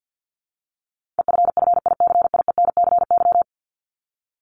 Morse code, heard as a keyed tone, about 52 wpm, 700 Hz. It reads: EFFICIENCY
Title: Top 1501-1750 Words 50wpm